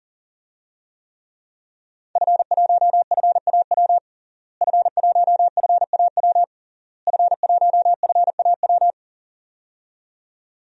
Morse code, heard as a keyed tone, about 40 wpm, 700 Hz. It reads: F1FAW F1FAW F1FAW
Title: DXCC-100 Call Signs - 5 Characters - Repeated 3x - 40wpm